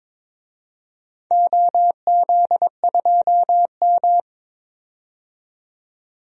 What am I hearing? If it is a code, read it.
Morse code, heard as a keyed tone, about 22 wpm, 700 Hz. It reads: OZ2M
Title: DXCC-100 Call Signs - 4 Characters - 22wpm